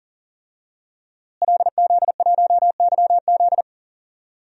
Morse code, heard as a keyed tone, about 40 wpm, 700 Hz. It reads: LZ1YZ
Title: DXCC-100 Call Signs - 5 Characters - Repeated 2x - 40wpm